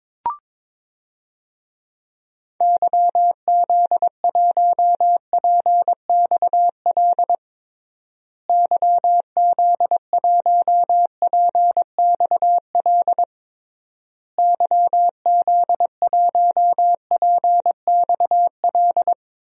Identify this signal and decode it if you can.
Morse code, heard as a keyed tone, about 22 wpm, 700 Hz. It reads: YZ1PXL YZ1PXL YZ1PXL
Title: DXCC-100 Call Signs - 6 Characters - Repeated 3x - 22wpm